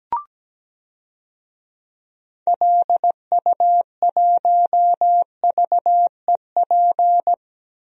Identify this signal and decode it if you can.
Morse code, heard as a keyed tone, about 17 wpm, 700 Hz. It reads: LU1VEP
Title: DXCC-100 Call Signs - 6 Characters - Repeated 2x - 17wpm